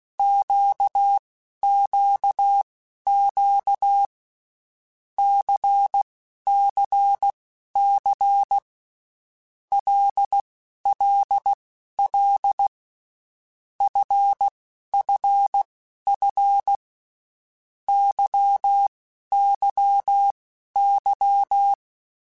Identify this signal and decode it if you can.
Morse code, heard as a keyed tone, about 16 wpm, spaced as for 11 wpm, 800 Hz. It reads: QQQ CCC LLL FFF YYY
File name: Groups@10 / Groups@10 mp3s / LFCYQ@10